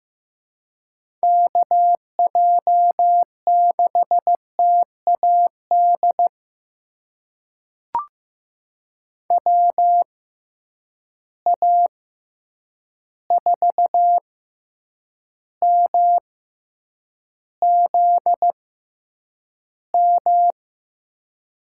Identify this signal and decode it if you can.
Morse code, heard as a keyed tone, about 16 wpm, spaced as for 8 wpm, 700 Hz. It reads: KJ6TAD WA4MZM
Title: USA Call Signs - 6 Characters - 6x Character Spacing - 15wpm